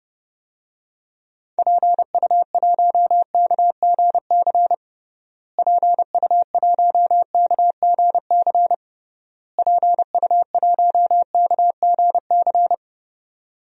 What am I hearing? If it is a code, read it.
Morse code, heard as a keyed tone, about 30 wpm, 700 Hz. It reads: PU1KGC PU1KGC PU1KGC
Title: DXCC-100 Call Signs - 6 Characters - Repeated 3x - 30wpm